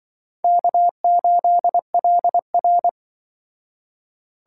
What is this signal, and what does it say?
Morse code, heard as a keyed tone, about 24 wpm, 700 Hz. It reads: K8LR
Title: Speed Racing - USA 1x2 Call Signs 20wpm